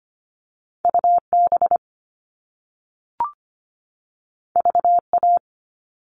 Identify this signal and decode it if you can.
Morse code, heard as a keyed tone, about 25 wpm, 700 Hz. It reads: UB VA
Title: All Cut Numbers - 2-Digit 25wpm